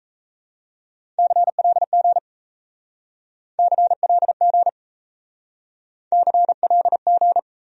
Morse code, heard as a keyed tone, about 37 wpm, 700 Hz. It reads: CLG CLG CLG